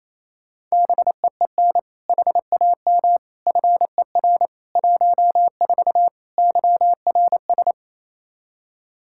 Morse code, heard as a keyed tone, about 28 wpm, 700 Hz. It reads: BEEN HAM FER 14 YRS
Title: Practice QSO - Exchange 3 - More Info - 28wpm